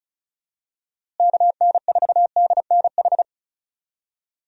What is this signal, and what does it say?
Morse code, heard as a keyed tone, about 34 wpm, 700 Hz. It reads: KN4DNH
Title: USA Call Signs - 6 Characters - Repeated 3x - 35wpm